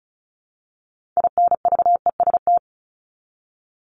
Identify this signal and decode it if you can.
Morse code, heard as a keyed tone, about 34 wpm, 700 Hz. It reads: INVEST
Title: Top 1001-1250 Words 35wpm